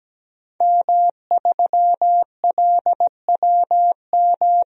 Morse code, heard as a keyed tone, about 17 wpm, 700 Hz. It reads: M3LWM
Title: DXCC-100 Call Signs - 5 Characters - Repeated 2x - 17wpm